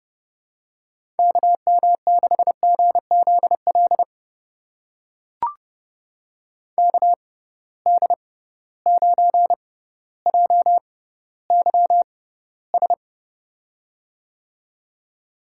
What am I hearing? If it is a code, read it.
Morse code, heard as a keyed tone, about 32 wpm, spaced as for 16 wpm, 700 Hz. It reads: KM6GZL KD9JYS
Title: USA Call Signs - 6 Characters - 6x Character Spacing - 30wpm